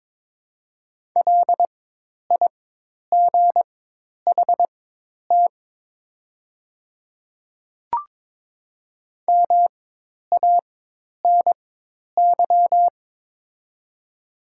Morse code, heard as a keyed tone, about 22 wpm, spaced as for 10 wpm, 690 Hz. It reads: LIGHT MANY
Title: 4x Character Spacing - Top 200 Words - 22wpm